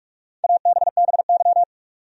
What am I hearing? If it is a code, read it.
Morse code, heard as a keyed tone, about 46 wpm, 700 Hz. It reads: ABBY
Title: USA Common Names - Max 4 Letters - 45wpm